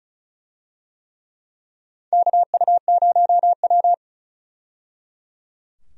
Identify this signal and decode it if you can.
Morse code, heard as a keyed tone, about 34 wpm, 700 Hz. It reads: KU0W
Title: USA Call Signs - 4 Characters - 35wpm